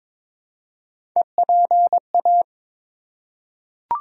Morse code, heard as a keyed tone, about 21 wpm, 700 Hz. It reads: EPA
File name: ARRL Sections v2 - 22wpm